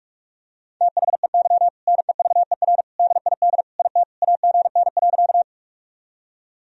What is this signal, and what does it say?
Morse code, heard as a keyed tone, about 46 wpm, 700 Hz. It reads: THEY NEVER DID IT AGN.
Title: Sentences from Top 300 Words and Abbreviations 45wpm